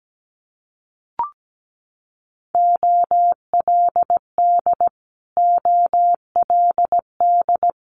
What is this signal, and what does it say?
Morse code, heard as a keyed tone, about 17 wpm, 700 Hz. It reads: OLD OLD
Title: Multi-letter Phonemes Repeated 3x 17wpm